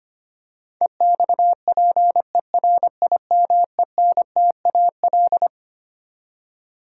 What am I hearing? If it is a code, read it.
Morse code, heard as a keyed tone, about 25 wpm, 700 Hz. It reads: EXPERIMENTAL
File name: Twelve Letter Words - 25wpm